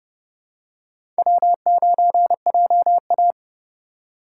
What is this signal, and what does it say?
Morse code, heard as a keyed tone, about 32 wpm, 700 Hz. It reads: W9JA
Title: USA Call Signs - 4x Character Spacing - 30wpm